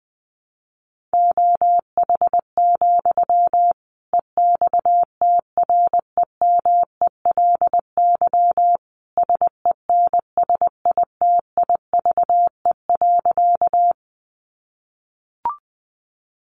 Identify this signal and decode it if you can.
Morse code, heard as a keyed tone, about 19 wpm, 700 Hz. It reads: OH, EXTREMELY SENSITIVE.
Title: Sentences from Top 1400 Words 20wpm